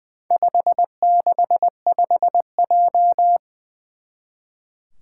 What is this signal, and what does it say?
Morse code, heard as a keyed tone, about 21 wpm, 700 Hz. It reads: 565J
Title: Four Challenging Letter-Number Combinations 20wpm